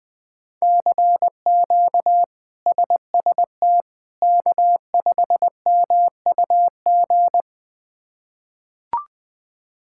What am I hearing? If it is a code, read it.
Morse code, heard as a keyed tone, about 20 wpm, 700 Hz. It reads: CQ SST K5MUG
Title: K1USN SST QSO Practice - 20wpm